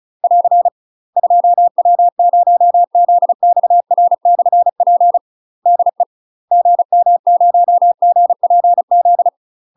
Morse code, heard as a keyed tone, about 36 wpm, 700 Hz. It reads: <AR> 2W0ZXR/P DE GM0GPZ
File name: UK MASTER QSOs TRAINING / 35WPM / UK MASTER QSOs 0010